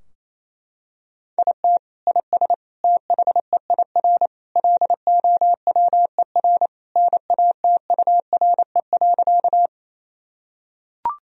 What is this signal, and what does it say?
Morse code, heard as a keyed tone, about 28 wpm, 700 Hz. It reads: IT IS THEIR LOWER NATURE.
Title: Sentences from Top 1000 Words - Review Part 4 - 28wpm